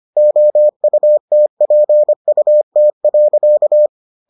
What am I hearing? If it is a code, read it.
Morse code, heard as a keyed tone, about 25 wpm, 600 Hz. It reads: OUTPUT.